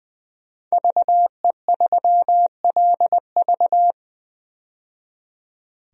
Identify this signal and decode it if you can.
Morse code, heard as a keyed tone, about 20 wpm, 700 Hz. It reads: VE3LV
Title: DXCC-100 Call Signs - 5 Characters - 20wpm